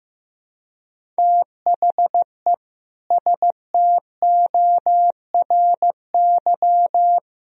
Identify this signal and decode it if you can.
Morse code, heard as a keyed tone, about 15 wpm, 700 Hz. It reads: THE STORY
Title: Sentences from Top 1000 Words - Review Part 1 - 15wpm